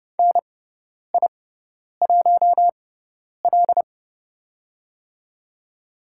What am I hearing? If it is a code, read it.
Morse code, heard as a keyed tone, about 30 wpm, spaced as for 10 wpm, 700 Hz. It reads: NI1L
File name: Lesson 16g - Letter c - Call Signs 30wpm